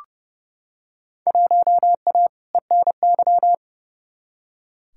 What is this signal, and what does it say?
Morse code, heard as a keyed tone, about 31 wpm, 700 Hz. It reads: 1A ENY
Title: Field Day Exchanges Encore - 30wpm